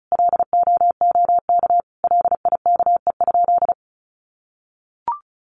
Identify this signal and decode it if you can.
Morse code, heard as a keyed tone, about 34 wpm, 700 Hz. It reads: LOOK LIKE?
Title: Sentences from Top 100 Words 35wpm